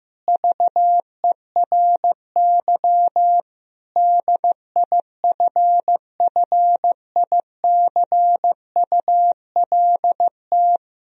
Morse code, heard as a keyed tone, about 15 wpm, 700 Hz. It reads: VERY DIFFICULT
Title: Sentences from Top 1000 Words - Review Part 2 - 15wpm